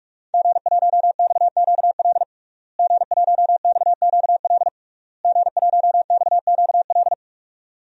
Morse code, heard as a keyed tone, about 46 wpm, 700 Hz. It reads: G1XQL G1XQL G1XQL
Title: DXCC-100 Call Signs - 5 Characters - Repeated 3x - 45wpm